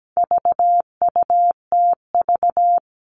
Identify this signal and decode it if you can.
Morse code, heard as a keyed tone, about 17 wpm, 700 Hz. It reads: VUTV
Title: All Cut Numbers - 4-Digit 17wpm